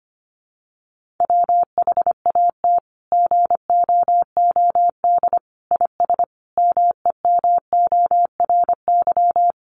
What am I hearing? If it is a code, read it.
Morse code, heard as a keyed tone, about 25 wpm, 700 Hz. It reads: WHAT GOOD IS MEMORY
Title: Sentences from Top 1300 Words 25wpm